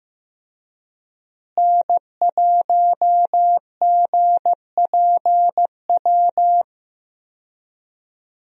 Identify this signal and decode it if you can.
Morse code, heard as a keyed tone, about 16 wpm, 700 Hz. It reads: N1GPW
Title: USA Call Signs - 5 Characters - Repeated 2x - 15wpm